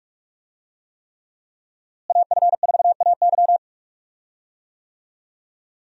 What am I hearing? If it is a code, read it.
Morse code, heard as a keyed tone, about 46 wpm, 700 Hz. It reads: AF4AY
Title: USA Call Signs - 5 Characters - Repeated 3x - 45wpm